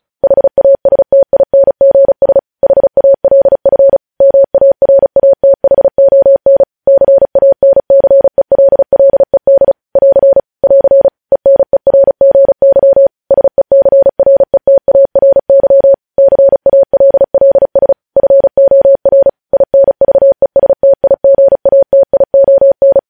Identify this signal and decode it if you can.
Morse code, heard as a keyed tone, about 36 wpm, 550 Hz. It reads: HASTINGS HALF MARATHON CANCELLED <AR> <AR> ENERGY SECRETARY CALLS FOR INVESTIGATION
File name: podcast / News headlines-1x-35x35